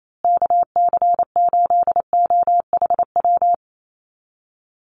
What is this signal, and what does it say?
Morse code, heard as a keyed tone, about 28 wpm, 700 Hz. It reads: KC8OHW
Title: USA Call Signs Repeated 3x 28wpm